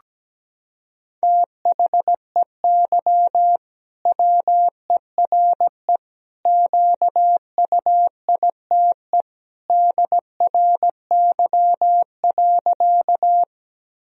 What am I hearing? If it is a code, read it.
Morse code, heard as a keyed tone, about 17 wpm, 700 Hz. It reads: THEY WERE QUITE DRY.
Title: Sentences from Top 1000 Words - Review Part 3 - 17wpm